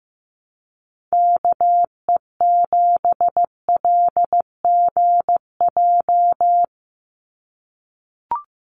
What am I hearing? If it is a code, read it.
Morse code, heard as a keyed tone, about 15 wpm, 700 Hz. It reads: KE7LGJ
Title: USA Call Signs 15wpm